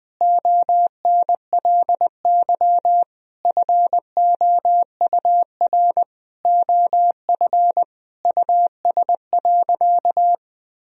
Morse code, heard as a keyed tone, about 20 wpm, 700 Hz. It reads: ONLY FOUR OF US.
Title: Sentences from Top 1000 Words - Review Part 1 - 20wpm